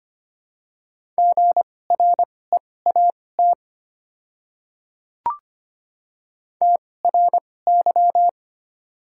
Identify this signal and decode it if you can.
Morse code, heard as a keyed tone, about 25 wpm, spaced as for 18 wpm, 700 Hz. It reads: GREAT TRY
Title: 2x Character Spacing - Top 300 Words - 25wpm